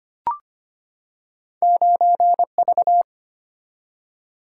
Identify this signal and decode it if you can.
Morse code, heard as a keyed tone, about 25 wpm, 700 Hz. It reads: 9V
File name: Two Letter-Number Combinations - 25wpm